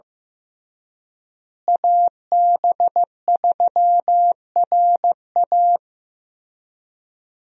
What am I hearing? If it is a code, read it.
Morse code, heard as a keyed tone, about 15 wpm, 700 Hz. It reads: AB3RA
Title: USA Call Signs - 5 Characters - Repeated 3x - 15wpm